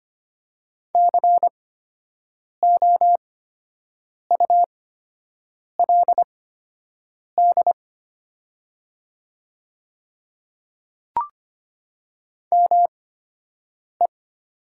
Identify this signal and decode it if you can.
Morse code, heard as a keyed tone, about 25 wpm, spaced as for 7 wpm, 700 Hz. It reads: COULD ME
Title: 8x Character Spacing - Top 200 Words - 25wpm